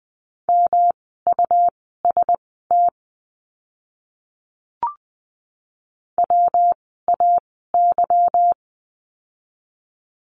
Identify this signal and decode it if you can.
Morse code, heard as a keyed tone, about 20 wpm, spaced as for 14 wpm, 700 Hz. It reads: MUST WAY
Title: 237 Lesson 38f - Letter q - Words 20wpm